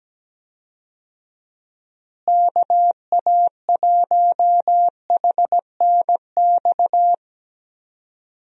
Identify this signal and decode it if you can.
Morse code, heard as a keyed tone, about 17 wpm, 700 Hz. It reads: KA1HNX
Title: USA Call Signs - 6 Characters - Repeated 2x - 17wpm